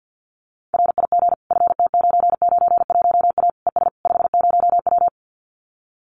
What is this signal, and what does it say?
Morse code, heard as a keyed tone, about 50 wpm, 700 Hz. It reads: RIG FT991A ES 50W
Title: Common QSO Phrases Encore 50wpm